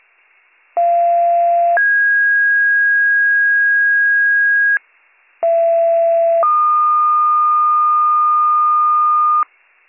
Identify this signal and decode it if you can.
Two-tone page, heard as a one-second tone, then a three-second tone, about 690 then 1650 Hz, about 670 then 1220 Hz, with noise. It is heard from 0.8 to 9.4 s.